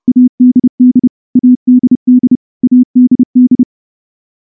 Morse code, heard as a keyed tone, about 28 wpm, 250 Hz. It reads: ADD ADD ADD
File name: morse-training-30WPM